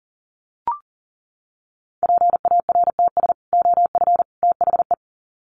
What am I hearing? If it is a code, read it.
Morse code, heard as a keyed tone, about 40 wpm, 700 Hz. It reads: PARTS OF THE
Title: Sets of 3 Words 40wpm